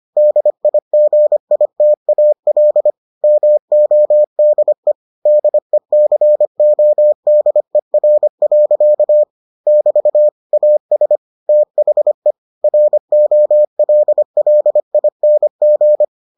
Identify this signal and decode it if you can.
Morse code, heard as a keyed tone, about 25 wpm, 600 Hz. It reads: DIGITAL MODE DECODER. <BT> AS THE ROLLING